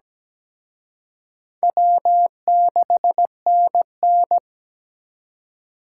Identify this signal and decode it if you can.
Morse code, heard as a keyed tone, about 20 wpm, 700 Hz. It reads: W6NN
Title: Speed Racing - USA Call Signs 17wpm